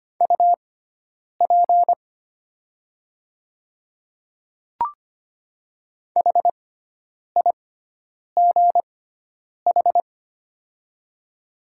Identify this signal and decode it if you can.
Morse code, heard as a keyed tone, about 25 wpm, spaced as for 9 wpm, 700 Hz. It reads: UP HIGH
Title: 6x Character Spacing - Top 200 Words - 25wpm